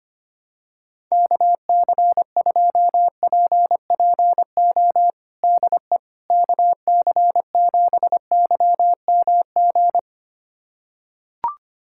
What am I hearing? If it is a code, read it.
Morse code, heard as a keyed tone, about 25 wpm, 700 Hz. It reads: KC2PPO DE KC7YMG
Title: USA Callsign DE Callsign - 6 Characters - 25wpm